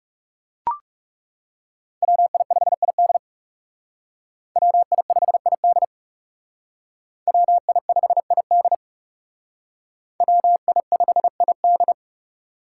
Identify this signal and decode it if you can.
Morse code, heard as a keyed tone, about 37 wpm, 700 Hz. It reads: WI5ID WI5ID WI5ID WI5ID
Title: Speed Racing - USA 2x2 Call Signs 30wpm